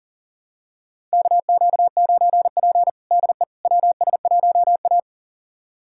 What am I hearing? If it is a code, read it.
Morse code, heard as a keyed tone, about 37 wpm, 700 Hz. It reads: KQ9P DE WS1A
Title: USA Callsign DE Callsign - 4 Characters - 40wpm